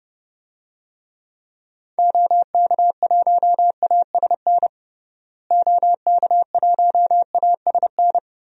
Morse code, heard as a keyed tone, about 29 wpm, 700 Hz. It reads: OK1ASN OK1ASN
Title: DXCC-100 Call Signs - 6 Characters - Repeated 3x - 30wpm